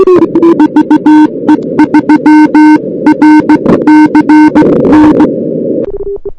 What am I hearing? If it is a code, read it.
Morse code, heard as a keyed tone, about 16 wpm, 300 Hz. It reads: VE3RCN
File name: CWSprint / Feb2006 / 2006-feb-cw-sprint-k5tr op-n5rz -02:00